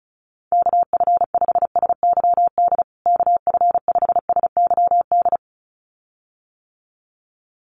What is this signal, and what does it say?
Morse code, heard as a keyed tone, about 34 wpm, 700 Hz. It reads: KF5SYD KF5SYD
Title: USA Call Signs Repeated 2x 35wpm